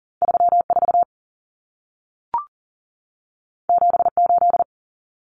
Morse code, heard as a keyed tone, about 40 wpm, 700 Hz. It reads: 34 78
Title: Numbers - 2-Digit 40wpm